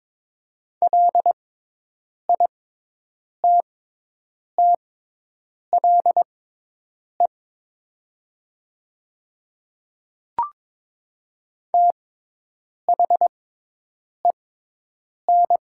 Morse code, heard as a keyed tone, about 22 wpm, spaced as for 8 wpm, 700 Hz. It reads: LITTLE THEN
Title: 6x Character Spacing - Top 200 Words - 22wpm